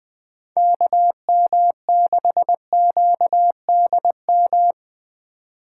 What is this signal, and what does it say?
Morse code, heard as a keyed tone, about 20 wpm, 700 Hz. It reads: KM6QDM
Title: USA Call Signs - 6 Characters - 2x Character Spacing - 20wpm